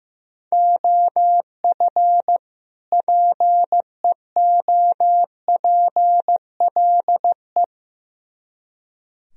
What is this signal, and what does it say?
Morse code, heard as a keyed tone, about 16 wpm, 700 Hz. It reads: OF PEOPLE
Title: Speed-Racing - Sets of 2 Words 15wpm